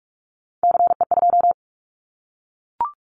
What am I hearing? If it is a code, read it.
Morse code, heard as a keyed tone, about 44 wpm, 700 Hz. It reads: CE2
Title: Callsign Prefixes for Top 100 DXCC - 45wpm